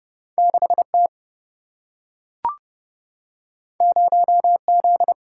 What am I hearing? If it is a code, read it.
Morse code, heard as a keyed tone, about 29 wpm, 700 Hz. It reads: 6T 0Z
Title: Single-Letter Single-Number 30wpm